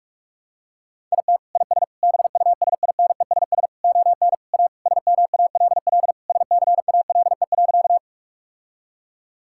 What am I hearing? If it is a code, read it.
Morse code, heard as a keyed tone, about 46 wpm, 700 Hz. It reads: IT IS BUSINESS ON A SMALL SCALE.
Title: Sentences from Top 1000 Words - Review Part 5 - 45wpm